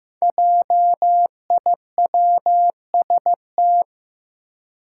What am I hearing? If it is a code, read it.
Morse code, heard as a keyed tone, about 15 wpm, 700 Hz. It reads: JIWST
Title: Five Challenging Letter-Number Combinations 15wpm